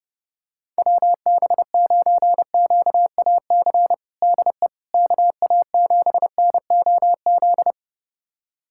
Morse code, heard as a keyed tone, about 30 wpm, 700 Hz. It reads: WB9QAC DE KA7NOZ
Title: USA Callsign DE Callsign - 6 Characters - 30wpm